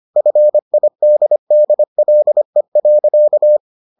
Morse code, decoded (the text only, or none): FIDDLE.